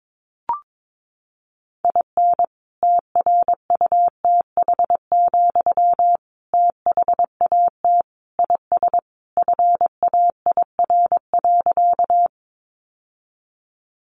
Morse code, heard as a keyed tone, about 22 wpm, 700 Hz. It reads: IN TRUTH, THAT IS FAIR.
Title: Sentences from Top 1300 Words 22wpm